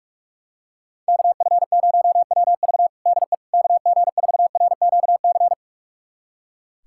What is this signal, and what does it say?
Morse code, heard as a keyed tone, about 46 wpm, 700 Hz. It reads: KF0WV DE KG4RQC
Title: USA Callsign DE Callsign 45wpm